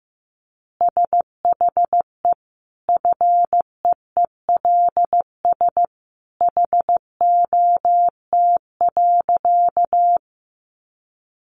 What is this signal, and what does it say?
Morse code, heard as a keyed tone, about 15 wpm, 700 Hz. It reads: SHE FEELS HOT.
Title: Easy Phrases 15wpm